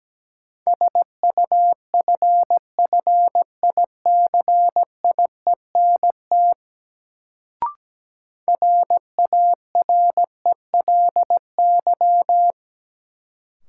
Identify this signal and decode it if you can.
Morse code, heard as a keyed tone, about 17 wpm, 700 Hz. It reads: SUFFICIENT RARELY
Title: Sentences from Top 1400 Words - New Word Drill Alt - 17wpm